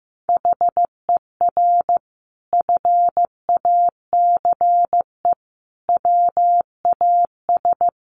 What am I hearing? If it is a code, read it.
Morse code, heard as a keyed tone, about 15 wpm, 700 Hz. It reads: HER FACE WAS
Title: Sentences from Top 1500 Words 15wpm